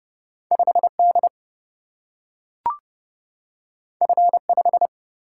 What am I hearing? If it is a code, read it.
Morse code, heard as a keyed tone, about 30 wpm, 700 Hz. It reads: 5D F5
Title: Single-Letter Single-Number 30wpm